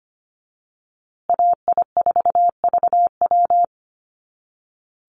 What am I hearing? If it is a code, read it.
Morse code, heard as a keyed tone, about 25 wpm, 700 Hz. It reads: AI4VW
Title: USA Call Signs Repeated 3x 25wpm